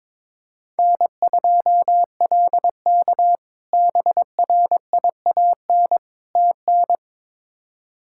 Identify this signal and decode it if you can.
Morse code, heard as a keyed tone, about 22 wpm, 700 Hz. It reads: N2LK BRIAN TN
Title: K1USN SST QSO Practice - 22wpm